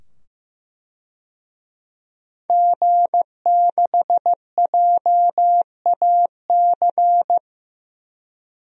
Morse code, heard as a keyed tone, about 15 wpm, 700 Hz. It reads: G6JAC
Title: DXCC-100 Call Signs - 5 Characters - 15wpm